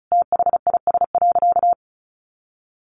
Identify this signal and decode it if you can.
Morse code, heard as a keyed tone, about 34 wpm, 700 Hz. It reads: THIS.
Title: Sentences from Top 1300 Words 35wpm